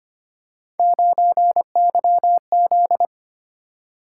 Morse code, heard as a keyed tone, about 25 wpm, 700 Hz. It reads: 9YZ
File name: Three Letter-Number Combinations - 25wpm